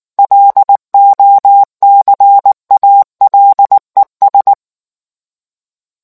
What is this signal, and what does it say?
Morse code, heard as a keyed tone, about 19 wpm, 800 Hz. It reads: LOCALES